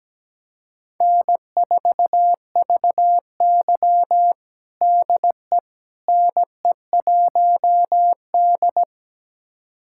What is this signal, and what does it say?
Morse code, heard as a keyed tone, about 17 wpm, 700 Hz. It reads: N4VY DE NE1D
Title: USA Callsign DE Callsign - 4 Characters - 17wpm